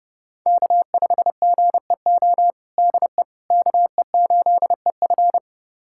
Morse code, heard as a keyed tone, about 30 wpm, 700 Hz. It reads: K5GEO DE KE8EF
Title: USA Callsign DE Callsign - 5 Characters - 30wpm